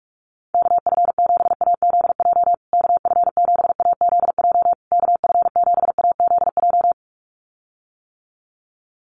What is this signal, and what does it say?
Morse code, heard as a keyed tone, about 44 wpm, 700 Hz. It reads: KF7AZJ KF7AZJ KF7AZJ
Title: USA Call Signs Repeated 3x 45wpm